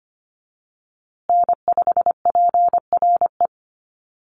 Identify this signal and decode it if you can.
Morse code, heard as a keyed tone, about 25 wpm, 700 Hz. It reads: N5PRE
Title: USA Call Signs Repeated 2x 25wpm